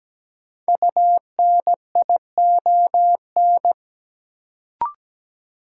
Morse code, heard as a keyed tone, about 17 wpm, 700 Hz. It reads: UNION
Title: Sentences from Top 1400 Words - New Word Drill - 17wpm